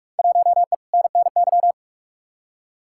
Morse code, heard as a keyed tone, about 46 wpm, 700 Hz. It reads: 1E NNY
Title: Field Day Exchanges - 45wpm